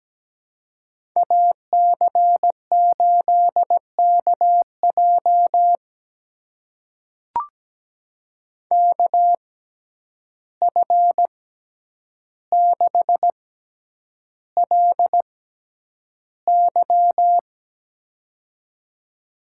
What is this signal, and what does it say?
Morse code, heard as a keyed tone, about 17 wpm, spaced as for 9 wpm, 700 Hz. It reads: AC8KJ KF6LY
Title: USA Call Signs - 5 Characters - 6x Character Spacing - 17wpm